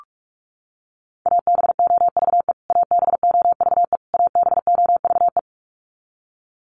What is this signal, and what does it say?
Morse code, heard as a keyed tone, about 46 wpm, 700 Hz. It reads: ABOVE ABOVE ABOVE
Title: Top 500 Words Repeated 45wpm